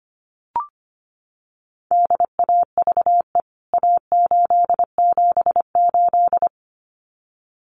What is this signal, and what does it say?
Morse code, heard as a keyed tone, about 25 wpm, 700 Hz. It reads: DAVE A878
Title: CWT Exchange 25wpm